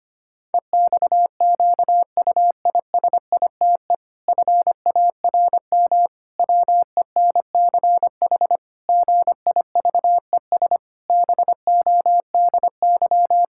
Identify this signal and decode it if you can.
Morse code, heard as a keyed tone, about 25 wpm, 700 Hz. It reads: EXQUISITE FARM WENCH GIVES BODY
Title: Pangram Challenge v2 - 25wpm